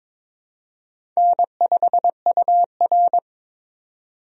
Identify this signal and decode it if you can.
Morse code, heard as a keyed tone, about 22 wpm, 700 Hz. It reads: N5UR
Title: USA Call Signs - 4 Characters - 22wpm